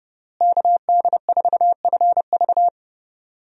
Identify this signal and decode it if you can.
Morse code, heard as a keyed tone, about 29 wpm, 700 Hz. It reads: KD4FV
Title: USA Call Signs - 5 Characters - Repeated 2x - 30wpm